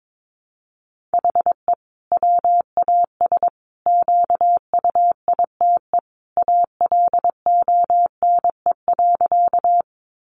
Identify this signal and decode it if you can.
Morse code, heard as a keyed tone, about 22 wpm, 700 Hz. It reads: HE WAS QUITE ALONE.
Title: Sentences from Top 1000 Words 22wpm